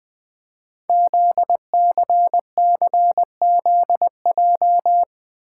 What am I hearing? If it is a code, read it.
Morse code, heard as a keyed tone, about 20 wpm, 700 Hz. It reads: ZCCZJ
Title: Code Groups - 5 Letters - 20wpm